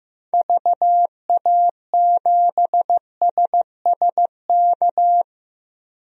Morse code, heard as a keyed tone, about 16 wpm, 700 Hz. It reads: VA7SSK
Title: DXCC-100 Call Signs - 6 Characters - 15wpm